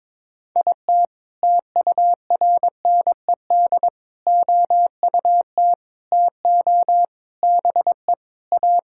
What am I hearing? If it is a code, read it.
Morse code, heard as a keyed tone, about 22 wpm, 700 Hz. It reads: IT TURNED OUT TO BE A
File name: practice / 22wpm / Sentences with 4-Word Phrases - 22wpm